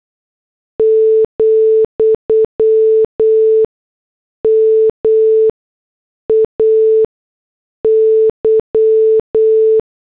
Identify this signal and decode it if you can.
Morse code, heard as a keyed tone, about 8 wpm, spaced as for 6 wpm, 440 Hz. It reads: , MAY
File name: morse / headlines